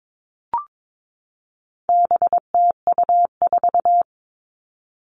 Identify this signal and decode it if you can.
Morse code, heard as a keyed tone, about 22 wpm, 700 Hz. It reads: BTU4
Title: All Cut Numbers - 4-Digit 22wpm